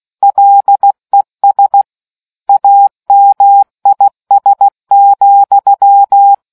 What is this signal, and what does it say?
Morse code, heard as a keyed tone, about 16 wpm, 800 Hz. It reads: LES AMIS,